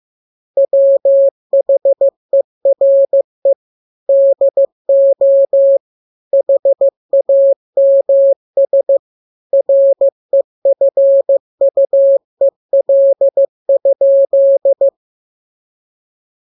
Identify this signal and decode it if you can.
Morse code, heard as a keyed tone, about 15 wpm, 550 Hz. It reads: WHERE DO HAMS REFUEL?
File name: jokes / joke 22 15 0000